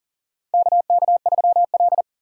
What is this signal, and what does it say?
Morse code, heard as a keyed tone, about 40 wpm, 700 Hz. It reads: KK3L
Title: USA Call Signs - 4 Characters - 6x Character Spacing - 40wpm